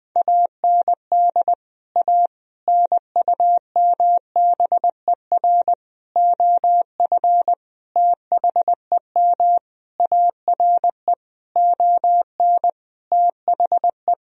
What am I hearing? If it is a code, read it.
Morse code, heard as a keyed tone, about 20 wpm, 700 Hz. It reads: AND A NUMBER OF THEM ARE ON THE
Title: Sentences from Top 1000 Words - Review Part 1 - 20wpm